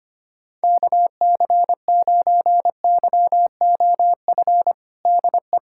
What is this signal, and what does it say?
Morse code, heard as a keyed tone, about 25 wpm, 700 Hz. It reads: KC9YOF DE
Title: USA Callsign DE Callsign 25wpm